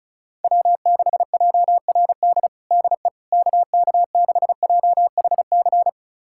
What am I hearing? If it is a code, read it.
Morse code, heard as a keyed tone, about 36 wpm, 690 Hz. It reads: W6JRD DE KK6JHC
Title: USA Callsign DE Callsign 35wpm